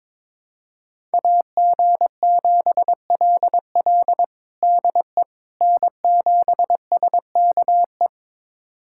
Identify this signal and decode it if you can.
Morse code, heard as a keyed tone, about 22 wpm, 700 Hz. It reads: AG7LL DE N7SKE
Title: USA Callsign DE Callsign - 5 Characters - 22wpm